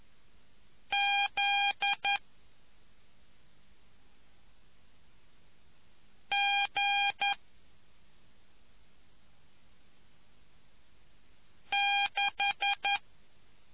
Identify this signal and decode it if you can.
Morse code, heard as a keyed tone, about 11 wpm, spaced as for 2 wpm, 2400 Hz. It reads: ZG6